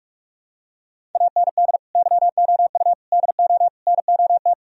Morse code, heard as a keyed tone, about 44 wpm, 700 Hz. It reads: AND YOU DO NOT